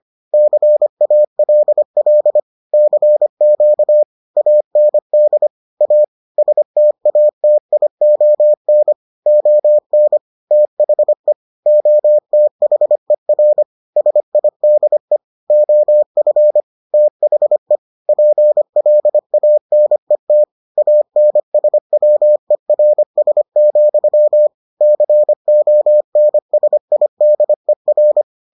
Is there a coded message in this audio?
Yes: Morse code from 0.3 to 28.2 s, heard as a keyed tone, about 25 wpm, 600 Hz. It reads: CALL CQ AND A STATION ON THE OTHER SIDE OF THE PLANET ANSWERS, CONSIDER